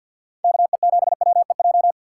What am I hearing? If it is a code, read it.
Morse code, heard as a keyed tone, about 50 wpm, 700 Hz. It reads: KE7WEJ
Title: USA Call Signs - 6 Characters - 4x Character Spacing - 50wpm